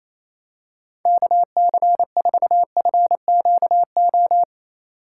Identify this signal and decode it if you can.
Morse code, heard as a keyed tone, about 28 wpm, 700 Hz. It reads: KC4FQO
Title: USA Call Signs - 3x Character Spacing - 28wpm